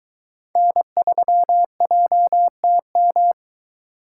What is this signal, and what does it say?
Morse code, heard as a keyed tone, about 23 wpm, 700 Hz. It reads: N3JTM